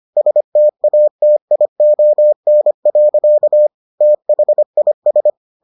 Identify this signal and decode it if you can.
Morse code, heard as a keyed tone, about 25 wpm, 600 Hz. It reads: STATION. THIS